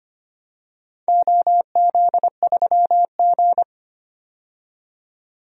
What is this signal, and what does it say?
Morse code, heard as a keyed tone, about 25 wpm, 700 Hz. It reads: OZ3G
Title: DXCC-100 Call Signs - 4 Characters - 25wpm